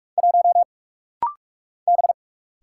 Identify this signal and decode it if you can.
Morse code, heard as a keyed tone, about 45 wpm, 700 Hz. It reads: 1 B
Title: Single Letter-Number Alt - Rapid-Fire 45wpm